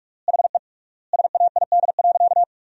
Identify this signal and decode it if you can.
Morse code, heard as a keyed tone, about 46 wpm, 700 Hz. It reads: HE SAID.
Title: 3x Word-Spacing - Sentences from Top 500 Words - 45wpm